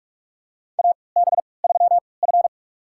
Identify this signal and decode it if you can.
Morse code, heard as a keyed tone, about 46 wpm, spaced as for 25 wpm, 700 Hz. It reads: AB3F
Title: USA Call Signs - 4 Characters - 3x Character Spacing - 45wpm